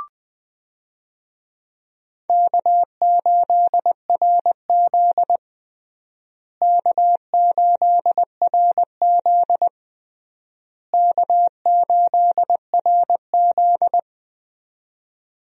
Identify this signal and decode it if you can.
Morse code, heard as a keyed tone, about 21 wpm, 700 Hz. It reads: K8RZ K8RZ K8RZ
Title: USA Call Signs - 4 Characters - Repeated 3x - 20wpm